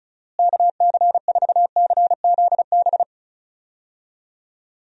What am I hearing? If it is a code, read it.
Morse code, heard as a keyed tone, about 36 wpm, 700 Hz. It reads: KC4CZB
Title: USA Call Signs - 6 Characters - Repeated 2x - 35wpm